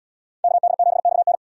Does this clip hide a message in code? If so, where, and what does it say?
Yes: Morse code from 0.4 to 1.4 s, heard as a keyed tone, about 72 wpm, 700 Hz. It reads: KD7XN